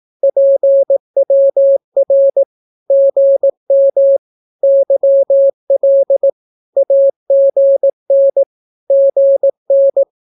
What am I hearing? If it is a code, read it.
Morse code, heard as a keyed tone, about 18 wpm, 550 Hz. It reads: PWR GM YL AGN GN
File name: naukapodstaw / skroty / skroty